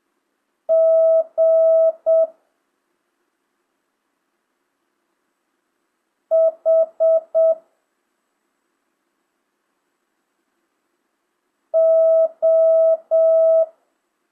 Morse code, heard as a keyed tone, about 7 wpm, 650 Hz. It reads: G H O
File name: jh8cbh / 2025cw